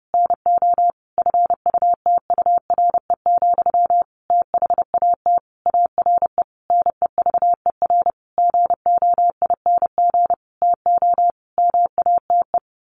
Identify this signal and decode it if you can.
Morse code, heard as a keyed tone, about 30 wpm, 700 Hz. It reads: NO FUTURE, THAT ARE NEVER GOING TO MATE
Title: A Sound Of Thunder 30wpm